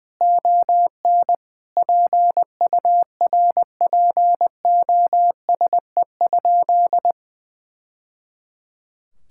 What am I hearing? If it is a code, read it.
Morse code, heard as a keyed tone, about 20 wpm, 700 Hz. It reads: ON PURPOSE?
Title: Sentences from Top 1000 Words - Review Part 5 - 20wpm